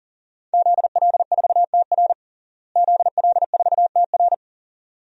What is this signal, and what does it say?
Morse code, heard as a keyed tone, about 40 wpm, 700 Hz. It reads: ZL4TR ZL4TR
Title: DXCC-100 Call Signs - 5 Characters - Repeated 2x - 40wpm